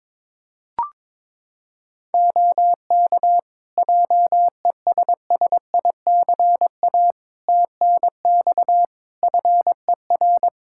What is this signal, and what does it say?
Morse code, heard as a keyed tone, about 22 wpm, 700 Hz. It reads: OK JESSICA TNX FER
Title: Practice QSO - Exchange 4 - Ending - 22wpm